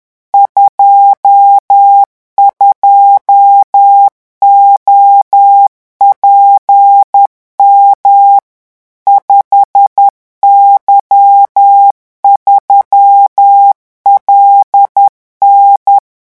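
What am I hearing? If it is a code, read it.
Morse code, heard as a keyed tone, about 10 wpm, 800 Hz. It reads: 22OPM 5Y3LN
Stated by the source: Morse revision exercise for French radio amateurs at 10 words per minute.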